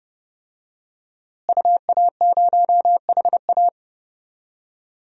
Morse code, heard as a keyed tone, about 30 wpm, 700 Hz. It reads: UA0HA
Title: DXCC-100 Call Signs - 5 Characters - Repeated 3x - 30wpm